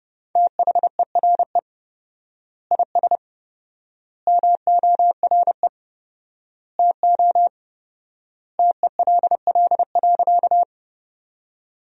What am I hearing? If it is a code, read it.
Morse code, heard as a keyed tone, about 30 wpm, 700 Hz. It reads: THERE IS MORE TO TELL.
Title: 4x Word-Spacing - Sentences from Top 200 Words - 30wpm